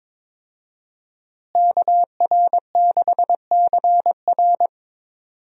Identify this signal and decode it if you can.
Morse code, heard as a keyed tone, about 22 wpm, 700 Hz. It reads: KR6CR
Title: USA Call Signs - 5 Characters - Repeated 3x - 22wpm